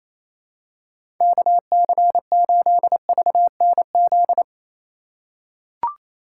Morse code, heard as a keyed tone, about 27 wpm, 700 Hz. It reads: KC8VNZ
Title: USA Call Signs - 6x Character Spacing - 28wpm